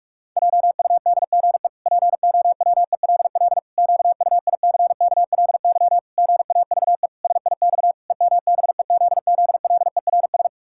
Morse code, heard as a keyed tone, about 46 wpm, 700 Hz. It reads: JUDGE POWELL QUICKLY GAVE SIX EMBEZZLERS
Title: Pangram Challenge v2 - 45wpm